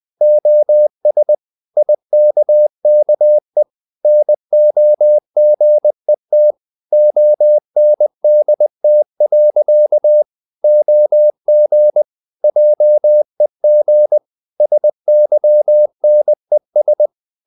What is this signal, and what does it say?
Morse code, heard as a keyed tone, about 20 wpm, 600 Hz. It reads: OS IKKE NOGET ONDT. OG JEG SYNES
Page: Ferie 20 wpm